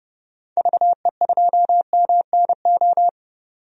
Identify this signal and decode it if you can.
Morse code, heard as a keyed tone, about 31 wpm, 700 Hz. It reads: VE2MNO
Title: DXCC-100 Call Signs - 6 Characters - 30wpm